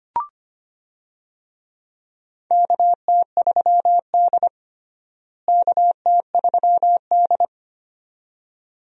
Morse code, heard as a keyed tone, about 25 wpm, 700 Hz. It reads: KT3D KT3D
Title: USA Call Signs - 4 Characters - Repeated 2x - 25wpm